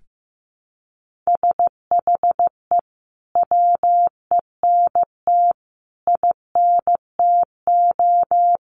Morse code, heard as a keyed tone, about 15 wpm, 700 Hz. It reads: SHE WENT INTO
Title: Sentences from Top 1400 Words 15wpm